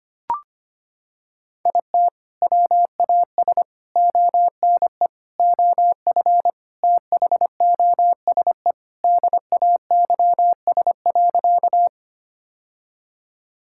Morse code, heard as a keyed tone, about 25 wpm, 700 Hz. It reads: IT WAS ONE OF THOSE DAYS.
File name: practice / 25wpm / Sentences with 3-Word Phrases - 25wpm